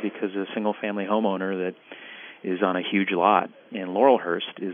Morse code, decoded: none

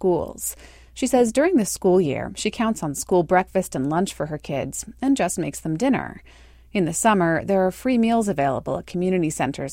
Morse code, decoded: none

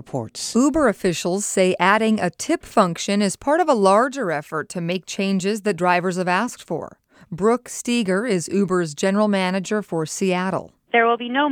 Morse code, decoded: none

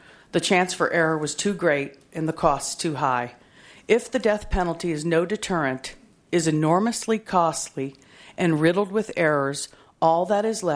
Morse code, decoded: none

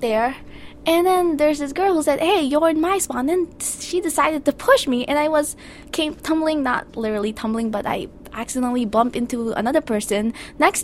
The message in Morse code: none

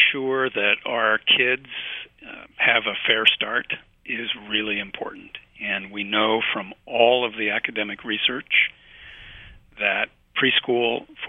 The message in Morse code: none

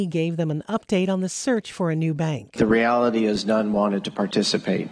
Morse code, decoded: none

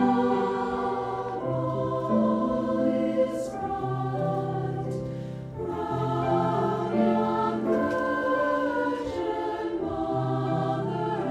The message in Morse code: none

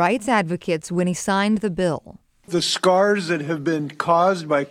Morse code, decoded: none

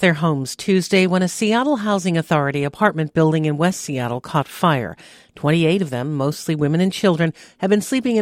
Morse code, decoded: none

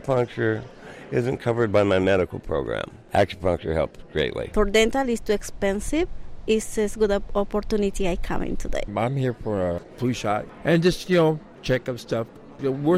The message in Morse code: none

